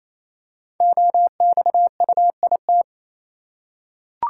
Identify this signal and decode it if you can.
Morse code, heard as a keyed tone, about 27 wpm, 700 Hz. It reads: OXUIT
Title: Code Groups - 5 Letters - 28wpm